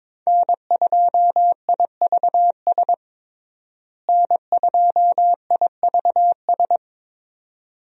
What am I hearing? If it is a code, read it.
Morse code, heard as a keyed tone, about 22 wpm, 700 Hz. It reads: N2IVS N2IVS
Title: USA Call Signs - 5 Characters - Repeated 2x - 22wpm